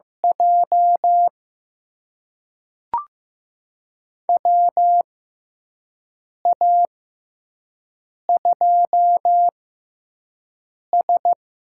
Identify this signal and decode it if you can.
Morse code, heard as a keyed tone, about 15 wpm, spaced as for 5 wpm, 700 Hz. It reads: J WA2S
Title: USA Call Signs - 4 Characters - 6x Character Spacing - 15wpm